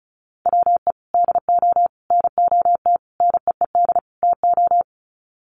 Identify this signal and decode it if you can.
Morse code, heard as a keyed tone, about 34 wpm, 700 Hz. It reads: WE DO NOT NEED TO
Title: Sets of 5 Words Encore 35wpm